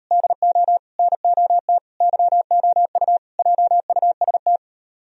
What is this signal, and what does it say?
Morse code, heard as a keyed tone, about 38 wpm, 700 Hz. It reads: DO NOT YOU JUST